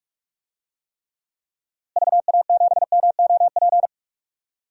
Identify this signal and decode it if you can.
Morse code, heard as a keyed tone, about 46 wpm, 700 Hz. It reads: VA7MOP
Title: DXCC-100 Call Signs - 6 Characters - Repeated 3x - 45wpm